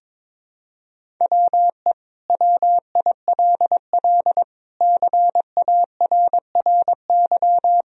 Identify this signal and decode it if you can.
Morse code, heard as a keyed tone, about 22 wpm, 700 Hz. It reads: WE WILL CARRY
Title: Sentences from Top 1000 Words - Review Part 1 - 22wpm